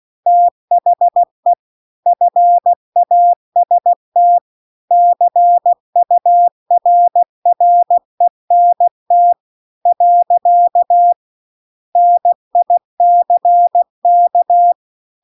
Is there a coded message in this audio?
Yes: Morse code from 0.3 to 14.7 s, heard as a keyed tone, about 16 wpm, 700 Hz. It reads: THE FAST CURRENT . NICK